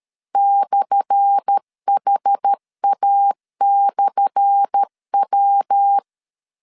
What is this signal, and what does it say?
Morse code, heard as a keyed tone, about 13 wpm, 800 Hz. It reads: /HA/W
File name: LEZIONI / 10 SIGNS / 10A SLASH 12wpm 2314